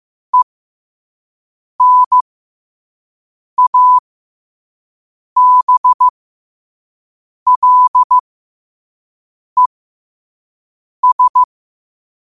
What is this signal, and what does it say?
Morse code, heard as a keyed tone, about 15 wpm, spaced as for 5 wpm, 1000 Hz. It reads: ENABLES
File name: qotdfiles / 5wpm